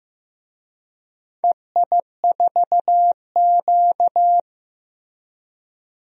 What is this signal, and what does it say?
Morse code, heard as a keyed tone, about 15 wpm, 700 Hz. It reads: EI4Q
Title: DXCC-100 Call Signs - 4 Characters - Repeated 3x - 15wpm